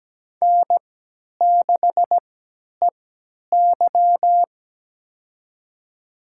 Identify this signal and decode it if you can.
Morse code, heard as a keyed tone, about 17 wpm, spaced as for 10 wpm, 700 Hz. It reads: N6EY
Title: USA Call Signs - 4 Characters - 3x Character Spacing - 17wpm